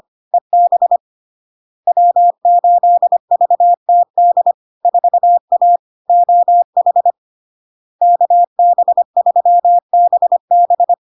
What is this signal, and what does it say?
Morse code, heard as a keyed tone, about 25 wpm, 700 Hz. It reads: EB W8VTD 4A OH KB3BB
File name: LARK-FD-2022-Log-in-CW-25wpm-700-Hz